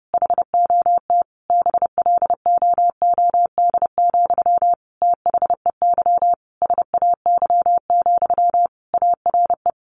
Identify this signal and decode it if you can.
Morse code, heard as a keyed tone, about 30 wpm, 700 Hz. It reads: HOT BLOOD, THEY SAY, ARE
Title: Sentences from Top 1100 Words 30wpm